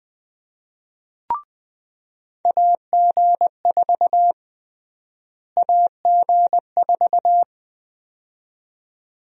Morse code, heard as a keyed tone, about 20 wpm, 700 Hz. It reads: AG4 AG4
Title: USA Call Sign Prefixes - Repeated 2x - 20wpm